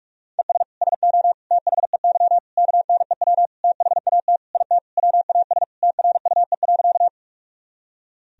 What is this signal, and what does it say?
Morse code, heard as a keyed tone, about 42 wpm, 700 Hz. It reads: ES SO THEY KNEW THAT IT WAS TRUE.
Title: Sentences from Top 400 Words and Abbreviations 45wpm